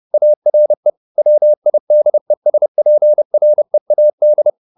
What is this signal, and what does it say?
Morse code, heard as a keyed tone, about 30 wpm, 600 Hz. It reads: ARE WIDESPREAD
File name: the old man and the sea / the old man and the sea 1 30wpm